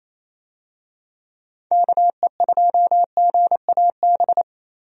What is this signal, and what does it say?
Morse code, heard as a keyed tone, about 27 wpm, 700 Hz. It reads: KE2GAB
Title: USA Call Signs - 6 Characters - 28wpm